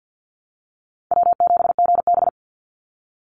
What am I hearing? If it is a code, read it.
Morse code, heard as a keyed tone, about 50 wpm, 700 Hz. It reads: W7GB
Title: Easy Call Signs Repeated 2x - 50wpm